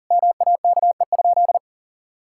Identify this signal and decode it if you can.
Morse code, heard as a keyed tone, about 40 wpm, 700 Hz. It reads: MAKE?
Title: Sentences from Top 1000 Words - Review Part 6 - 40wpm